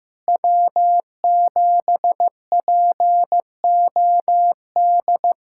Morse code, heard as a keyed tone, about 15 wpm, 700 Hz. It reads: W7POD
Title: SOTA QSO Practice 15wpm